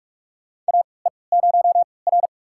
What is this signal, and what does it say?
Morse code, heard as a keyed tone, about 43 wpm, spaced as for 25 wpm, 700 Hz. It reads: AE0R
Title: USA Call Signs - 4 Characters - 3x Character Spacing - 45wpm